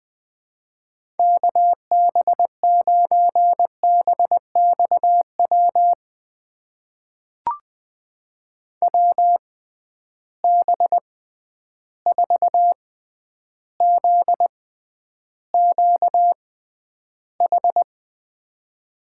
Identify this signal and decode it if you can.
Morse code, heard as a keyed tone, about 20 wpm, spaced as for 10 wpm, 700 Hz. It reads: KB9BXW WB4ZQH
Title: USA Call Signs - 6 Characters - 6x Character Spacing - 20wpm